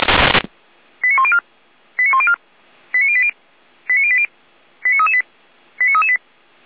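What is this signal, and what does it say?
Five-tone selective call (ZVEI), heard as quick runs of five tones, tones of about 70 ms. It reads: 89104 89104 89080 89080 89338 89338
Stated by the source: ZVEI five-tone selective call 89270 received by SDR.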